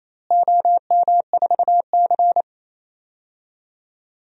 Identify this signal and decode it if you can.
Morse code, heard as a keyed tone, about 28 wpm, 700 Hz. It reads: OM4C
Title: DXCC-100 Call Signs - 4 Characters - Repeated 3x - 28wpm